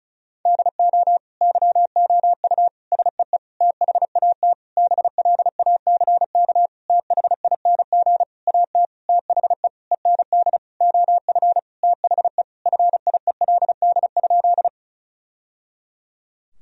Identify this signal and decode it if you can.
Morse code, heard as a keyed tone, about 36 wpm, 700 Hz. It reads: DO YOU SEE THAT BLACK THING AT THE END OF THE FIELD?
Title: Sentences from Top 1000 Words - Review Part 2 - 35wpm